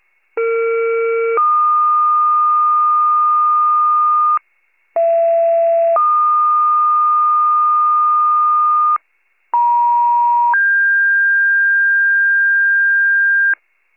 Two-tone page, heard as a one-second tone, then a three-second tone, about 460 then 1220 Hz, about 670 then 1220 Hz, about 950 then 1600 Hz. It is heard from 0.4 to 13.5 s.
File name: wcalert